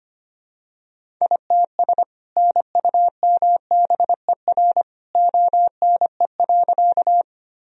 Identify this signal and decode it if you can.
Morse code, heard as a keyed tone, about 25 wpm, 700 Hz. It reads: ITS NUMBER ONE.
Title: Sentences from Top 1000 Words - Review Part 1 - 25wpm